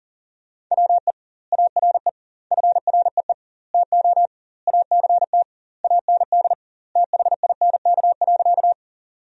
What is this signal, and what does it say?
Morse code, heard as a keyed tone, about 40 wpm, 700 Hz. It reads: WE ARE FREE TO ACT AND THINK.
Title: 2x Word-Spacing - Sentences from Top 500 Words - 40wpm